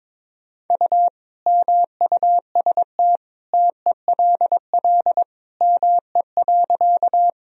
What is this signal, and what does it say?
Morse code, heard as a keyed tone, about 22 wpm, 700 Hz. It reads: U MUST TELL ME.
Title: Sentences from Top 200 Words and Abbreviations 22wpm